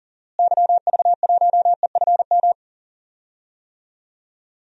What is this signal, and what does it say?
Morse code, heard as a keyed tone, about 40 wpm, 700 Hz. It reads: YV1EFM
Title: DXCC-100 Call Signs - 6 Characters - Repeated 3x - 40wpm